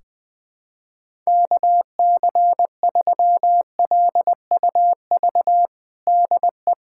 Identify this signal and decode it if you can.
Morse code, heard as a keyed tone, about 21 wpm, 700 Hz. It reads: KC3LUV DE
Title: USA Callsign DE Callsign 20wpm